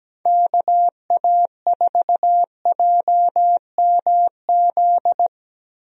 Morse code, heard as a keyed tone, about 17 wpm, 700 Hz. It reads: KA4JMZ
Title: USA Call Signs - 6 Characters - 17wpm